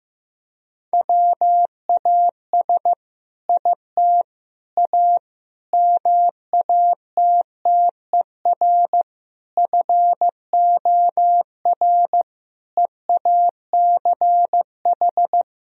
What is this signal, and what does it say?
Morse code, heard as a keyed tone, about 15 wpm, 700 Hz. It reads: WAS IT A MATTER FOR EACH
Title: Sentences from Top 1000 Words - Review Part 4 - 15wpm